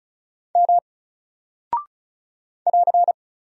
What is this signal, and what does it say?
Morse code, heard as a keyed tone, about 36 wpm, 700 Hz. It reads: M <AR>
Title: Single Character (Letter:Number:Punctuation:Prosign) 35wpm